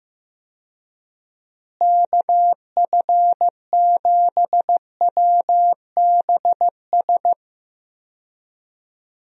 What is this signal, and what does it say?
Morse code, heard as a keyed tone, about 15 wpm, 700 Hz. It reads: KF7WBS
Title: USA Call Signs - 6 Characters - Repeated 3x - 15wpm